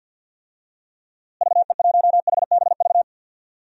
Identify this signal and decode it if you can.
Morse code, heard as a keyed tone, about 48 wpm, 700 Hz. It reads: VE1HBV
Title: DXCC-100 Call Signs - 6 Characters - Repeated 3x - 50wpm